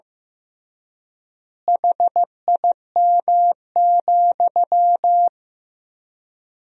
Morse code, heard as a keyed tone, about 15 wpm, 700 Hz. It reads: HIM,
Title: 3x Word-Spacing - Sentences from Top 300 Words - 15wpm